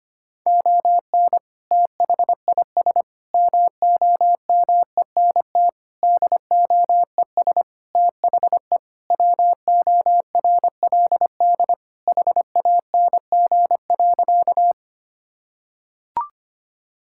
Morse code, heard as a keyed tone, about 25 wpm, 700 Hz. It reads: ON THIS MOMENT DOES THE WORLD HANG.
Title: Sentences from Top 1600 Words 25wpm